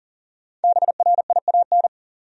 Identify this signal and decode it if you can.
Morse code, heard as a keyed tone, about 39 wpm, 700 Hz. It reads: BRIAN
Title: USA Common Names - Max 5 Letters - 40wpm